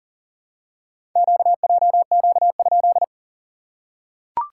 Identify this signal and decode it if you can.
Morse code, heard as a keyed tone, about 43 wpm, 690 Hz. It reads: QJQ?
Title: SOTA QSO Practice 40wpm